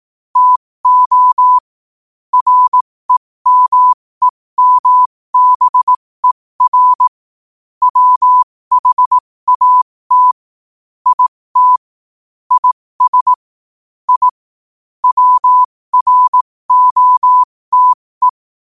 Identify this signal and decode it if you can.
Morse code, heard as a keyed tone, about 18 wpm, spaced as for 15 wpm, 1000 Hz. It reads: TO REMEMBER WHAT IT IS I WROTE
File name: qotdfiles / 15wpm